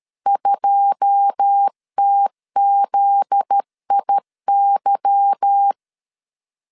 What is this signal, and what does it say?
Morse code, heard as a keyed tone, about 13 wpm, 800 Hz. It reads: 2TZIY
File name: CW / DET / 1 9 1500C S A 12wpm 3800